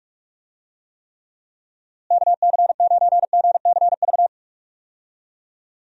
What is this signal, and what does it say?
Morse code, heard as a keyed tone, about 46 wpm, 700 Hz. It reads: KC9GCV
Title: USA Call Signs - 6 Characters - Repeated 2x - 45wpm